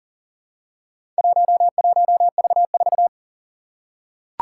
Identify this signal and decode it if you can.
Morse code, heard as a keyed tone, about 40 wpm, 700 Hz. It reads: 11V4
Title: Four Challenging Letter-Number Combinations 40wpm